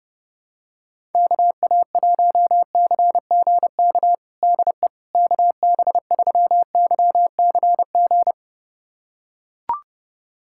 Morse code, heard as a keyed tone, about 30 wpm, 700 Hz. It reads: KA1CGK DE KB3YCG
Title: USA Callsign DE Callsign - 6 Characters - 30wpm